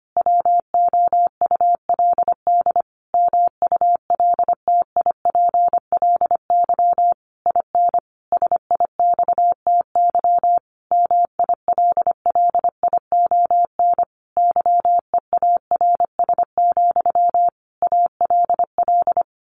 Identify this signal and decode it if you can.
Morse code, heard as a keyed tone, about 25 wpm, 700 Hz. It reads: WOULD MULTIPLY IN SIXTY MILLION YEARS, ALL
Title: A Sound Of Thunder 25wpm